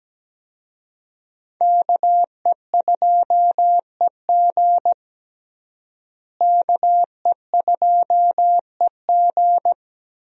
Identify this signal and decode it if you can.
Morse code, heard as a keyed tone, about 17 wpm, 700 Hz. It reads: KE2EG KE2EG
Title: USA Call Signs - 5 Characters - Repeated 2x - 17wpm